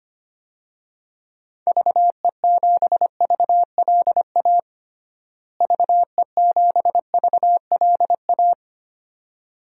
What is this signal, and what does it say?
Morse code, heard as a keyed tone, about 25 wpm, 700 Hz. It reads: VE7VLA VE7VLA
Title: DXCC-100 Call Signs - 6 Characters - Repeated 2x - 25wpm